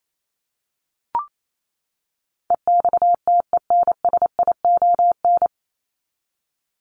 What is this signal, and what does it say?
Morse code, heard as a keyed tone, about 28 wpm, 700 Hz. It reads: EXTENSION
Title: Top 1501-1750 Words 28wpm